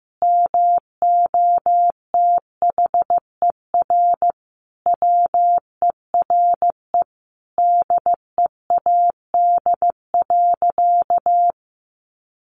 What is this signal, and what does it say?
Morse code, heard as a keyed tone, about 15 wpm, 700 Hz. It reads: MOTHER WERE DEAD.
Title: Sentences from Top 1500 Words 15wpm